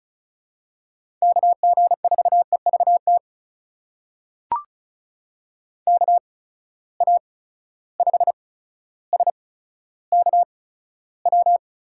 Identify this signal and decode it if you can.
Morse code, heard as a keyed tone, about 36 wpm, spaced as for 15 wpm, 700 Hz. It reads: KG4EVT KA5SKW
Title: USA Call Signs - 6 Characters - 8x Character Spacing - 35wpm